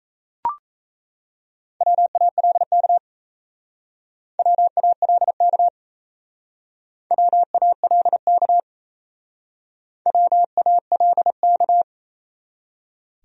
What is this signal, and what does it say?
Morse code, heard as a keyed tone, about 34 wpm, 700 Hz. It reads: WALK WALK WALK WALK
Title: Speed-Racing Top 500 Words 28wpm